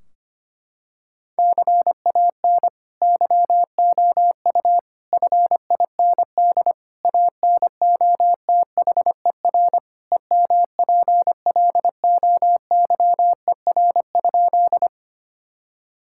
Morse code, heard as a keyed tone, about 25 wpm, 700 Hz. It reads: CAN YOU FIND ANOTHER EMPLOYER?
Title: Sentences from Top 1600 Words 25wpm